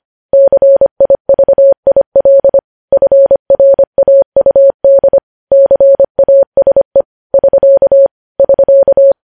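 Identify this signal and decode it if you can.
Morse code, heard as a keyed tone, about 25 wpm, 550 Hz. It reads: CIVIL FRAUD CASE <SK> <SK>
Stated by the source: Morse code news headlines at 25, opening with vvv.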